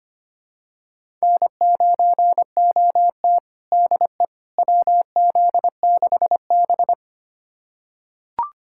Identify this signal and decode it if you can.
Morse code, heard as a keyed tone, about 25 wpm, 700 Hz. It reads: N9OT DE WZ6B
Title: USA Callsign DE Callsign - 4 Characters - 25wpm